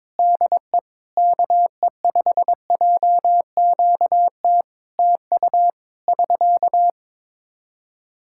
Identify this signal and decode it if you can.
Morse code, heard as a keyed tone, about 22 wpm, 700 Hz. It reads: DE KE5JQT TU <SK>
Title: K1USN SST QSO Practice 22wpm